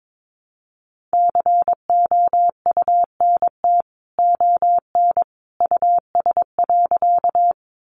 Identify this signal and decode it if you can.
Morse code, heard as a keyed tone, about 22 wpm, 700 Hz. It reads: COUNT ON US.
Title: Sentences from Top 1500 Words 22wpm